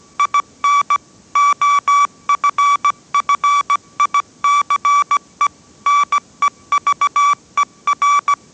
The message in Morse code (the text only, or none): IN OFFICE NEVER